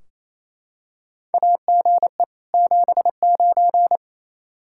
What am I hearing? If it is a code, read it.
Morse code, heard as a keyed tone, about 27 wpm, 700 Hz. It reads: AGE 79
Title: K1USN SST QSO Practice 28wpm